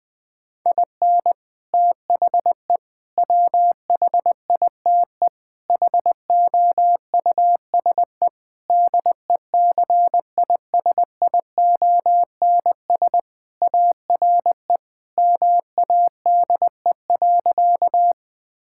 Morse code, heard as a keyed tone, about 21 wpm, 700 Hz. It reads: IN THE WHITE HOUSE DECISIONS ARE MADE.
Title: Sentences with 4-Word Phrases - 20wpm